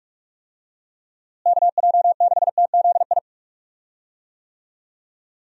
Morse code, heard as a keyed tone, about 46 wpm, 700 Hz. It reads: KJ6TZI
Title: USA Call Signs - 6 Characters - Repeated 2x - 45wpm